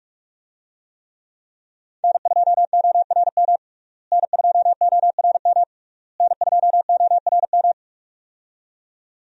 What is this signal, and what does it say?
Morse code, heard as a keyed tone, about 46 wpm, 700 Hz. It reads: N2ORM N2ORM N2ORM
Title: USA Call Signs - 5 Characters - Repeated 3x - 45wpm